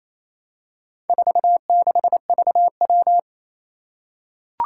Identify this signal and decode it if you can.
Morse code, heard as a keyed tone, about 27 wpm, 700 Hz. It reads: 46VW
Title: Four Challenging Letter-Number Combinations 28wpm